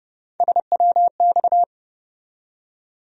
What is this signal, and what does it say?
Morse code, heard as a keyed tone, about 32 wpm, 700 Hz. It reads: SWX
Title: USA Call Sign Postfixes - 3 Letters 30wpm